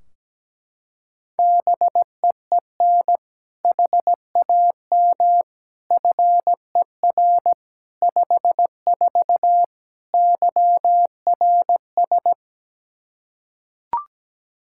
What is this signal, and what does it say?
Morse code, heard as a keyed tone, about 17 wpm, 700 Hz. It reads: BEEN HAM FER 54 YRS
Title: Practice QSO - Exchange 3 - More Info - 17wpm